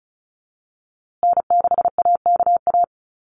Morse code, heard as a keyed tone, about 34 wpm, 700 Hz. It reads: N6AKA
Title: USA Call Signs Repeated 2x 35wpm